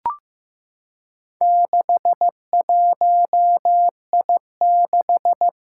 Morse code, heard as a keyed tone, about 15 wpm, 700 Hz. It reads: 61I6
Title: Four Challenging Letter-Number Combinations 15wpm